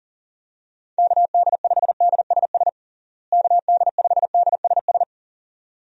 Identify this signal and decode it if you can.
Morse code, heard as a keyed tone, about 40 wpm, 700 Hz. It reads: KD5DSS KD5DSS
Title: USA Call Signs - 6 Characters - Repeated 2x - 40wpm